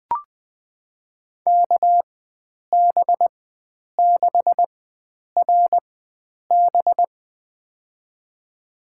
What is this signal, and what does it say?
Morse code, heard as a keyed tone, about 20 wpm, spaced as for 9 wpm, 700 Hz. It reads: KB6RB
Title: USA Call Signs - 5 Characters - 4x Character Spacing - 20wpm